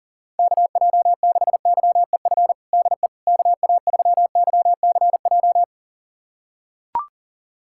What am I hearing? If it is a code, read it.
Morse code, heard as a keyed tone, about 37 wpm, 700 Hz. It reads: KJ6YEF DE KA3YCJ
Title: USA Callsign DE Callsign 40wpm